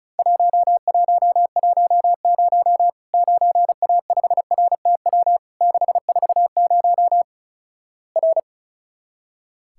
Morse code, heard as a keyed tone, about 36 wpm, 700 Hz. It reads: 1110 9A5RTW 640 R
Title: Worked All Europe DX Contest - QTC Exchange - 35wpm